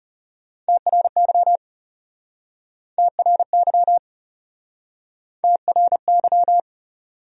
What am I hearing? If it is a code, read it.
Morse code, heard as a keyed tone, about 35 wpm, 700 Hz. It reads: TRY TRY TRY
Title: Speed-Racing Top 300 Words 30wpm